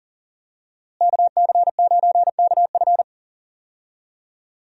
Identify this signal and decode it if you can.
Morse code, heard as a keyed tone, about 40 wpm, 700 Hz. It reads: KC9KF
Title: USA Call Signs - 5 Characters - Repeated 2x - 40wpm